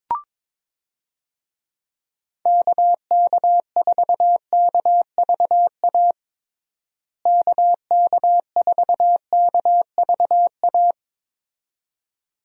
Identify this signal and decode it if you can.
Morse code, heard as a keyed tone, about 22 wpm, 700 Hz. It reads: KK4KVA KK4KVA
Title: USA Call Signs - 6 Characters - Repeated 2x - 22wpm